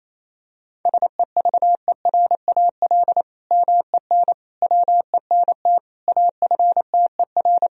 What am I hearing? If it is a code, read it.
Morse code, heard as a keyed tone, about 28 wpm, 700 Hz. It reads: SEVERAL MEN WENT AFTER
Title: Sentences from Top 1000 Words - Review Part 2 - 28wpm